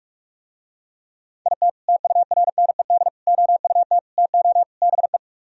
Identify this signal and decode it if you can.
Morse code, heard as a keyed tone, about 45 wpm, 700 Hz. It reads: IT TURNED OUT TO BE